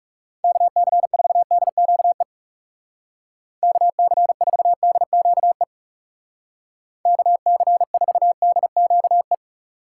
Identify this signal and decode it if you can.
Morse code, heard as a keyed tone, about 36 wpm, 700 Hz. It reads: KC4DQE KC4DQE KC4DQE
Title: Speed Racing - USA 2x3 Call Signs 30wpm